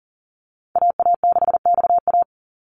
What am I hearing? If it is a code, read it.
Morse code, heard as a keyed tone, about 40 wpm, 700 Hz. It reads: AA6XA
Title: NA SOTA Activator Callsigns 40wpm